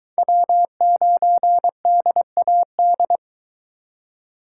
Morse code, heard as a keyed tone, about 23 wpm, 700 Hz. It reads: W9DAD